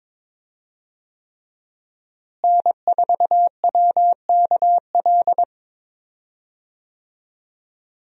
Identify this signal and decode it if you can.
Morse code, heard as a keyed tone, about 22 wpm, 700 Hz. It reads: N4WKL
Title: USA Call Signs - 5 Characters - Repeated 3x - 22wpm